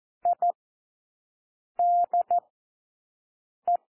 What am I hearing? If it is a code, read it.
Morse code, heard as a keyed tone, about 14 wpm, spaced as for 6 wpm, 700 Hz. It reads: IDE
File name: cwmp3 / W1 SOUND WORDS 14 wpm 0